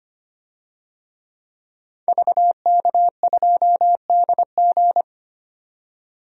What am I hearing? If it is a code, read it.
Morse code, heard as a keyed tone, about 25 wpm, 700 Hz. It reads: VK2DG
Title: DXCC-100 Call Signs - 5 Characters - 25wpm